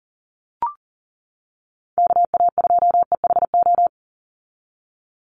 Morse code, heard as a keyed tone, about 40 wpm, 700 Hz. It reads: KA2EHO
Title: USA Call Signs 40wpm